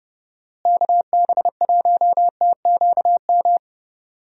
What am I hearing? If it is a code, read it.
Morse code, heard as a keyed tone, about 30 wpm, 700 Hz. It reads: KB1TQM
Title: USA Call Signs - 6 Characters - Repeated 3x - 30wpm